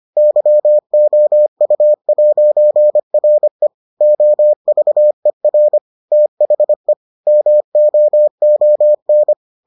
Morse code, heard as a keyed tone, about 25 wpm, 600 Hz. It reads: YOU'RE OVER THE MOON